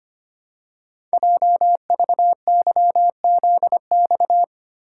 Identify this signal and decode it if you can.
Morse code, heard as a keyed tone, about 25 wpm, 700 Hz. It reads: JVYZX
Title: Code Groups - 5 Letters - 25wpm